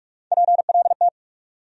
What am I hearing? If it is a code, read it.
Morse code, heard as a keyed tone, about 46 wpm, 700 Hz. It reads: PLT
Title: USA Call Sign Postfixes - 3 Letters 45wpm